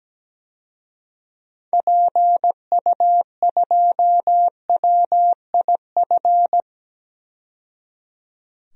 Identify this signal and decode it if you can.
Morse code, heard as a keyed tone, about 17 wpm, 700 Hz. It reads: PU2WIF
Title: DXCC-100 Call Signs - 6 Characters - 17wpm